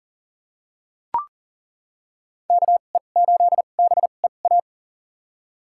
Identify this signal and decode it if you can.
Morse code, heard as a keyed tone, about 40 wpm, spaced as for 29 wpm, 700 Hz. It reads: KE8BEA
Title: USA Call Signs - 6 Characters - 2x Character Spacing - 40wpm